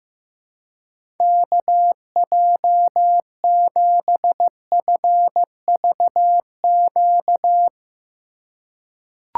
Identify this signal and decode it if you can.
Morse code, heard as a keyed tone, about 15 wpm, 700 Hz. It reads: KJ7FVQ
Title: USA Call Signs - 6 Characters - 2x Character Spacing - 15wpm